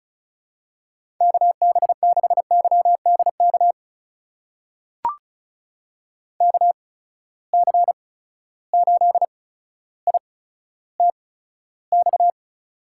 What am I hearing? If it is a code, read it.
Morse code, heard as a keyed tone, about 34 wpm, spaced as for 15 wpm, 700 Hz. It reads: KB6YDK KC8ITX
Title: USA Call Signs - 6 Characters - 8x Character Spacing - 35wpm